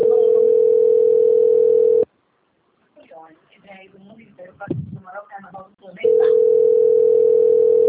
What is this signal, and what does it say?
Ringback tone (North America), 440 + 480 Hz, with speech.
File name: out- - - - - .50864